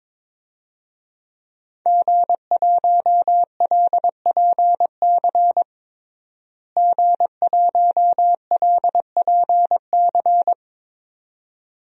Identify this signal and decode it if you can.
Morse code, heard as a keyed tone, about 22 wpm, 700 Hz. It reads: G1LPC G1LPC
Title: DXCC-100 Call Signs - 5 Characters - Repeated 2x - 22wpm